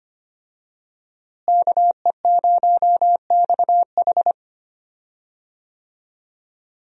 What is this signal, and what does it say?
Morse code, heard as a keyed tone, about 25 wpm, 700 Hz. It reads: KE0XH
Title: USA Call Signs - 5 Characters - Repeated 2x - 25wpm